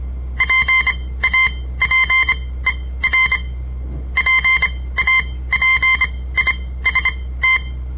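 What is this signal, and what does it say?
Morse code, heard as a keyed tone, about 26 wpm, spaced as for 17 wpm, 2100 Hz. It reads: PAPER PAPIST
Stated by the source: On-air Morse code practice broadcast by a southern Montana amateur radio association; all code, no read back.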